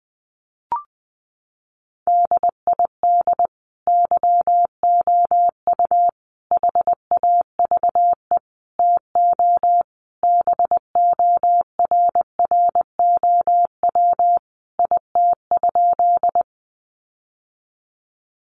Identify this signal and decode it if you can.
Morse code, heard as a keyed tone, about 20 wpm, 700 Hz. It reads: DID YOU HAVE TO BORROW IT?
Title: Sentences from Top 1500 Words 20wpm